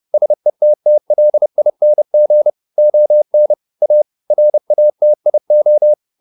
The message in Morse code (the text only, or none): SETTLING ON A RATIO